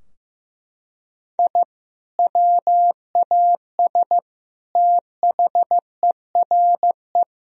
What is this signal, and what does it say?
Morse code, heard as a keyed tone, about 16 wpm, 700 Hz. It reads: I WAS THERE
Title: Sentences from Top 1000 Words - Review Part 2 - 15wpm